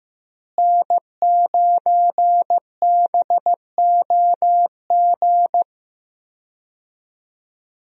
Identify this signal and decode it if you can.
Morse code, heard as a keyed tone, about 16 wpm, 700 Hz. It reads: N9BOG
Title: USA Call Signs - 5 Characters - Repeated 3x - 15wpm